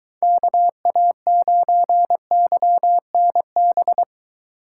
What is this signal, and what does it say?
Morse code, heard as a keyed tone, about 24 wpm, 700 Hz. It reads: KA9YNB